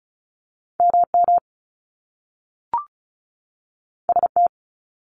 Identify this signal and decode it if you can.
Morse code, heard as a keyed tone, about 35 wpm, 700 Hz. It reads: M M S T
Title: Top Two Letter Combinations in Words 35wpm